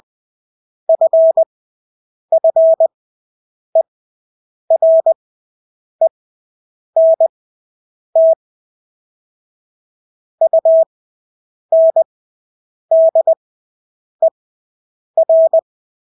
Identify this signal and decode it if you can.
Morse code, heard as a keyed tone, about 20 wpm, spaced as for 8 wpm, 650 Hz. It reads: FFERENT UNDER
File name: Engelske ord 20/8 wpm